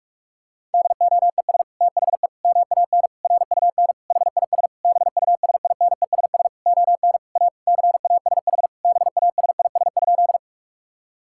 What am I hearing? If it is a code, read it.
Morse code, heard as a keyed tone, about 46 wpm, 700 Hz. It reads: DOES THE MAN RUN HIS BUSINESS ON A CASH BASIS?
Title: Sentences from Top 1000 Words - Review Part 5 - 45wpm